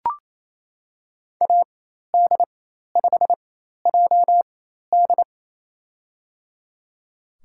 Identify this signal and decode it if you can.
Morse code, heard as a keyed tone, about 27 wpm, spaced as for 13 wpm, 700 Hz. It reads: AD5JD
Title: USA Call Signs - 4x Character Spacing - 28wpm